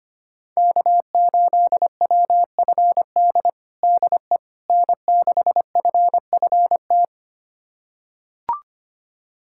Morse code, heard as a keyed tone, about 25 wpm, 700 Hz. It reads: K8WFD DE N6FFT
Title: USA Callsign DE Callsign - 5 Characters - 25wpm